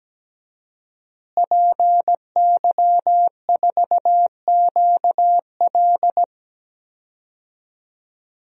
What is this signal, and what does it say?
Morse code, heard as a keyed tone, about 17 wpm, 700 Hz. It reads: PY4QL
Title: DXCC-100 Call Signs - 5 Characters - Repeated 2x - 17wpm